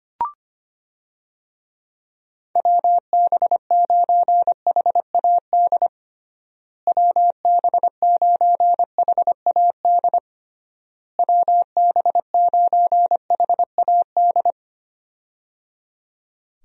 Morse code, heard as a keyed tone, about 25 wpm, 700 Hz. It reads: WB9HAD WB9HAD WB9HAD
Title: USA Call Signs - 6 Characters - Repeated 3x - 25wpm